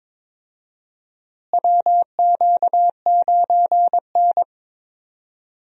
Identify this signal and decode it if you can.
Morse code, heard as a keyed tone, about 22 wpm, 700 Hz. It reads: WQ9N
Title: USA Call Signs - 4 Characters - Repeated 2x - 22wpm